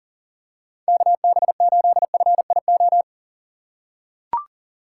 Morse code, heard as a keyed tone, about 40 wpm, 700 Hz. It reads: KB8FIO
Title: USA Call Signs - 6 Characters - 4x Character Spacing - 40wpm